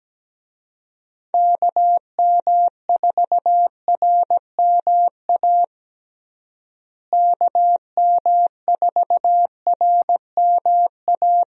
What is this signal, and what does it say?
Morse code, heard as a keyed tone, about 17 wpm, 700 Hz. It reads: KM4RMA KM4RMA
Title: USA Call Signs - 6 Characters - Repeated 2x - 17wpm